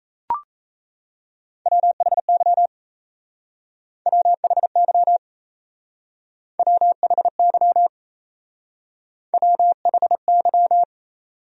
Morse code, heard as a keyed tone, about 34 wpm, 700 Hz. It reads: WHY WHY WHY WHY
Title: Speed-Racing Top 500 Words 28wpm